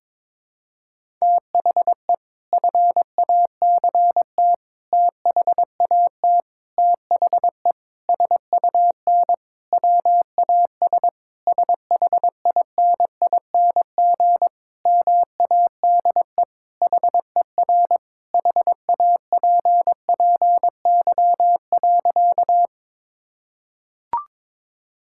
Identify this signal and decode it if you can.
Morse code, heard as a keyed tone, about 22 wpm, 700 Hz. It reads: THE FACT THAT THE SUN WAS SHINING MADE HER HAPPY.
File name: practice / 22wpm / Sentences with 4-Word Phrases - 22wpm